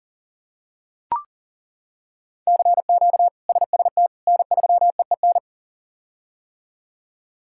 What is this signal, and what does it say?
Morse code, heard as a keyed tone, about 40 wpm, 700 Hz. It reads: CQ SST N3EEN
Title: K1USN SST QSO Practice - 40wpm